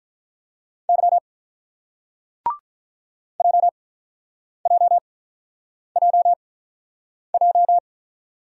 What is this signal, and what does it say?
Morse code, heard as a keyed tone, about 46 wpm, spaced as for 9 wpm, 700 Hz. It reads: Y JJJJ